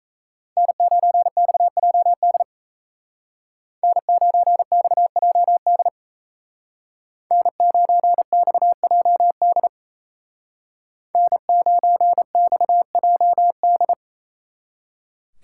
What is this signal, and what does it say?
Morse code, heard as a keyed tone, about 33 wpm, 700 Hz. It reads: N9XJD N9XJD N9XJD N9XJD
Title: Speed Racing - USA 1x3 Call Signs 28wpm